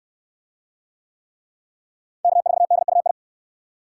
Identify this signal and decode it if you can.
Morse code, heard as a keyed tone, about 64 wpm, 700 Hz. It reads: K4DUI